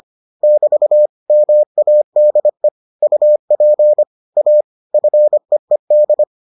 Morse code, heard as a keyed tone, about 25 wpm, 600 Hz. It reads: <BT> MADE UP A FEED